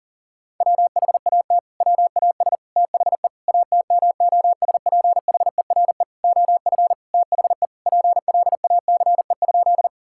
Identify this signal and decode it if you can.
Morse code, heard as a keyed tone, about 40 wpm, 700 Hz. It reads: WHAT WAS THE ATMOSPHERE OF THE PLACE?
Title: Sentences from Top 1600 Words 40wpm